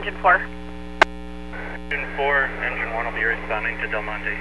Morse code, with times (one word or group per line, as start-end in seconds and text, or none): none